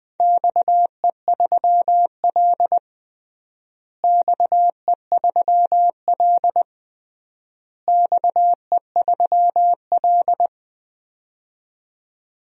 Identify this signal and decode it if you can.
Morse code, heard as a keyed tone, about 20 wpm, 700 Hz. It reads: XE3L XE3L XE3L
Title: DXCC-100 Call Signs - 4 Characters - Repeated 3x - 20wpm